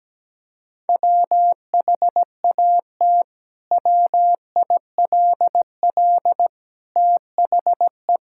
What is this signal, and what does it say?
Morse code, heard as a keyed tone, about 17 wpm, 700 Hz. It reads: WHAT WILL THE
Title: Sentences from Top 1000 Words - Review Part 5 - 17wpm